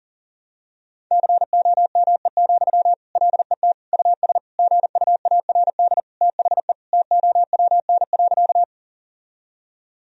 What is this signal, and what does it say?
Morse code, heard as a keyed tone, about 40 wpm, 700 Hz. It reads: COME, LET US GUARD THE TOWN.
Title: Sentences from Top 1000 Words - Review Part 4 - 40wpm